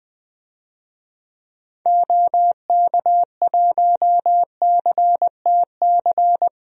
Morse code, heard as a keyed tone, about 20 wpm, 700 Hz. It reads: OK1CTC
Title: DXCC-100 Call Signs - 6 Characters - 20wpm